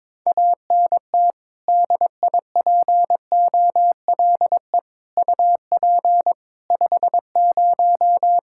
Morse code, heard as a keyed tone, about 22 wpm, 700 Hz. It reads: ANT DIPOLE UP 50
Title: Practice QSO - Exchange 2 - Info - 22wpm